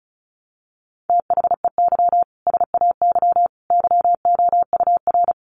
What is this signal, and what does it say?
Morse code, heard as a keyed tone, about 34 wpm, 700 Hz. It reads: THEY SAY YOUR
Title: Sentences from Top 1100 Words 35wpm